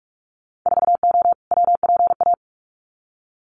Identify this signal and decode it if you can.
Morse code, heard as a keyed tone, about 44 wpm, 700 Hz. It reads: 4O WPA
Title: Winter Field Day Exchanges 45wpm